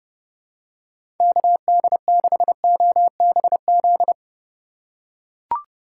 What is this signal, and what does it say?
Morse code, heard as a keyed tone, about 30 wpm, 700 Hz. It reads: KD6OBZ
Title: USA Call Signs - 6 Characters - 3x Character Spacing - 30wpm